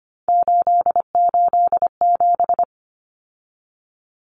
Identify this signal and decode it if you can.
Morse code, heard as a keyed tone, about 25 wpm, 700 Hz. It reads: 887
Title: Common Cut Numbers - 3-Digit 25wpm